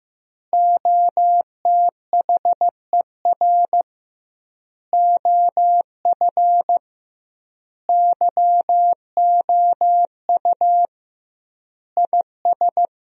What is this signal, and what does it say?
Morse code, heard as a keyed tone, about 15 wpm, 700 Hz. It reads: OTHER OF YOU IS
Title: 2x Word-Spacing - Sentences from Top 400 Words - 15wpm